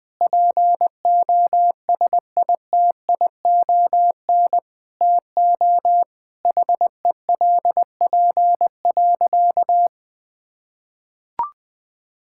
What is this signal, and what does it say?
Morse code, heard as a keyed tone, about 21 wpm, 700 Hz. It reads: POSITION TO HELP.
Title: Sentences with 7-Word Phrases - 20wpm